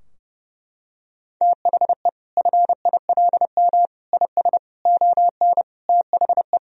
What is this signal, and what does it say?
Morse code, heard as a keyed tone, about 30 wpm, 700 Hz. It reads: THE FILM IS ON THE
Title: Sentences from Top 1000 Words - Review Part 3 - 30wpm